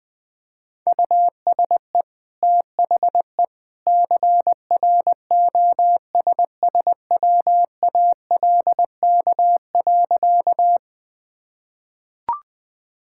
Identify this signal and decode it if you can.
Morse code, heard as a keyed tone, about 20 wpm, 700 Hz. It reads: USE THE CROSSWALK.
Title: Easy Sentences with Hints - 20wpm